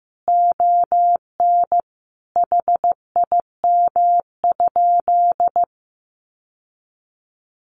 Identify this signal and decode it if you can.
Morse code, heard as a keyed tone, about 15 wpm, 700 Hz. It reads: ON HIM?
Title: Sentences from Top 1100 Words 15wpm